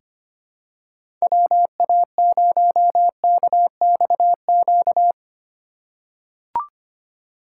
Morse code, heard as a keyed tone, about 25 wpm, 690 Hz. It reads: WA0KXQ
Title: USA Call Signs - 6 Characters - 3x Character Spacing - 25wpm